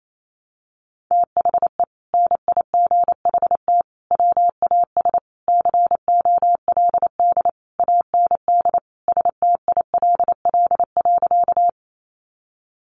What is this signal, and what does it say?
Morse code, heard as a keyed tone, about 28 wpm, 700 Hz. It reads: THE NIGHT WAS COLD AND STILL.
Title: Sentences from Top 400 Words 28wpm